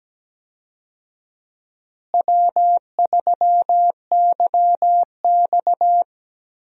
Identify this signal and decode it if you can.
Morse code, heard as a keyed tone, about 17 wpm, 700 Hz. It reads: W3YX
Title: USA Call Signs - 4 Characters - Repeated 3x - 17wpm